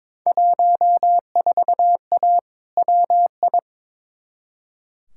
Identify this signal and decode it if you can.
Morse code, heard as a keyed tone, about 22 wpm, 700 Hz. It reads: 14A WI
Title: Field Day Exchanges Encore - 22wpm